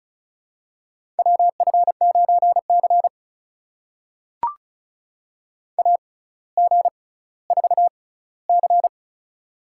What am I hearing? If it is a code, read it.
Morse code, heard as a keyed tone, about 36 wpm, spaced as for 18 wpm, 700 Hz. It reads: WF9C AG4C
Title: USA Call Signs - 4 Characters - 6x Character Spacing - 35wpm